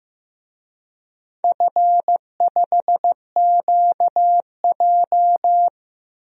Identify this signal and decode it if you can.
Morse code, heard as a keyed tone, about 15 wpm, 700 Hz. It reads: F5QJ
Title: DXCC-100 Call Signs - 4 Characters - Repeated 3x - 15wpm